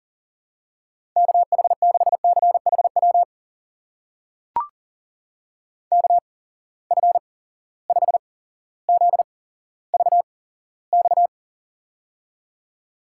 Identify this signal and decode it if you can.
Morse code, heard as a keyed tone, about 40 wpm, spaced as for 17 wpm, 700 Hz. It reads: KH6CHW KF5ZVX
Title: USA Call Signs - 6 Characters - 8x Character Spacing - 40wpm